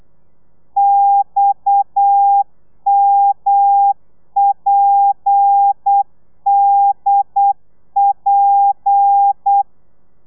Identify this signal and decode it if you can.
Morse code, heard as a keyed tone, about 8 wpm, 800 Hz. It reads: XMPDP